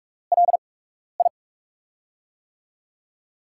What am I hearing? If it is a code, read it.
Morse code, heard as a keyed tone, about 44 wpm, spaced as for 12 wpm, 700 Hz. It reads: LI
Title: USA Call Signs - 8x Character Spacing - 45wpm